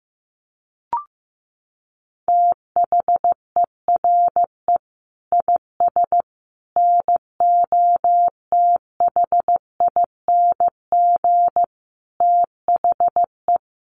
Morse code, heard as a keyed tone, about 15 wpm, 700 Hz. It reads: THERE IS NOTHING THE
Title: Sentences from Top 700 Words 15wpm